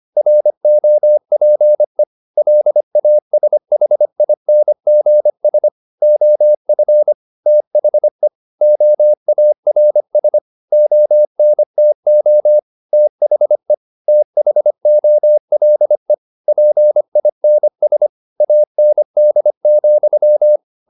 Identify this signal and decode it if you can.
Morse code, heard as a keyed tone, about 25 wpm, 600 Hz. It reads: ROPE LASHINGS OF THE OARS ONTO THE THOLE PINS AND,
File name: the old man and the sea part 1 25wpm